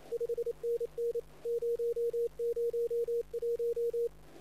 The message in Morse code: 5NN 001

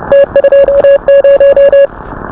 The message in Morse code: T20

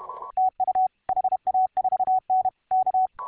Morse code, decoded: TU HA4N K

